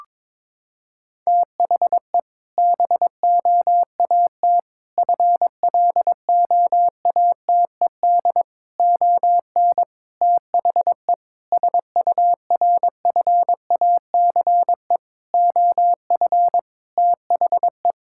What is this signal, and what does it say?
Morse code, heard as a keyed tone, about 22 wpm, 700 Hz. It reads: THE BOAT FLOATED ON THE SURFACE OF THE
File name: practice / 22wpm / Sentences with 4-Word Phrases - 22wpm